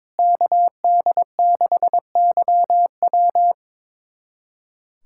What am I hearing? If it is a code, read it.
Morse code, heard as a keyed tone, about 21 wpm, 700 Hz. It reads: KD6YW
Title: USA Callsign DE Callsign - 5 Characters - 22wpm